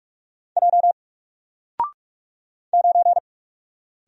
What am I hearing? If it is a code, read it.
Morse code, heard as a keyed tone, about 44 wpm, 700 Hz. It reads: J 9
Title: Single Letter-Number Alt 45wpm